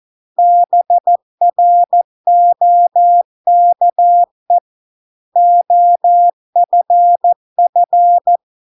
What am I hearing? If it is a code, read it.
Morse code, heard as a keyed tone, about 14 wpm, 700 Hz. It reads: BROKE OFF